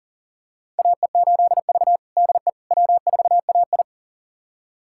Morse code, heard as a keyed tone, about 38 wpm, 690 Hz. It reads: AE8V DE W4AI
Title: USA Callsign DE Callsign - 4 Characters - 40wpm